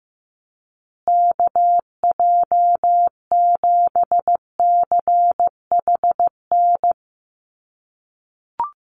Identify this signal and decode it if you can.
Morse code, heard as a keyed tone, about 15 wpm, 700 Hz. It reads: KJ7CHN
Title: USA Call Signs Repeated 3x 15wpm